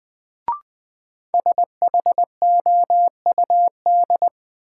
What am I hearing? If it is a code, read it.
Morse code, heard as a keyed tone, about 21 wpm, 700 Hz. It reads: SHOUD
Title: Sentences from Top 1000 Words - Review Part 4 - New Word Drill - 20wpm